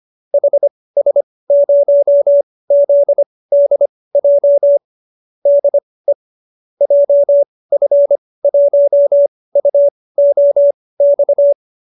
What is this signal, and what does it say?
Morse code, heard as a keyed tone, about 25 wpm, spaced as for 18 wpm, 590 Hz. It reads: HS0ZDJ DE JF1UOX